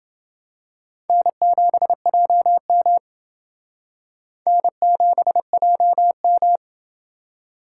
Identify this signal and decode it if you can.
Morse code, heard as a keyed tone, about 29 wpm, 700 Hz. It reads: N7JM N7JM